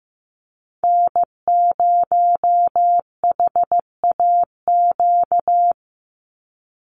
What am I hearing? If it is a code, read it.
Morse code, heard as a keyed tone, about 15 wpm, 700 Hz. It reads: N0HAQ
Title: USA Call Signs 15wpm